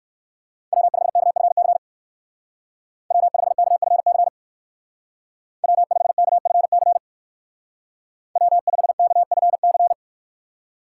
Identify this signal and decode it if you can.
Morse code, heard as a keyed tone, about 61 wpm, 700 Hz. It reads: W5KFC W5KFC W5KFC W5KFC